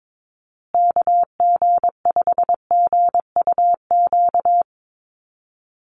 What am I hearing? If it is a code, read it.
Morse code, heard as a keyed tone, about 22 wpm, 700 Hz. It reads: KG5GUQ
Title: USA Call Signs Repeated 3x 22wpm